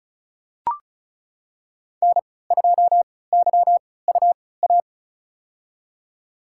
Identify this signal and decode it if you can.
Morse code, heard as a keyed tone, about 34 wpm, spaced as for 20 wpm, 700 Hz. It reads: N2YUA
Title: USA Call Signs - 5 Characters - 3x Character Spacing - 35wpm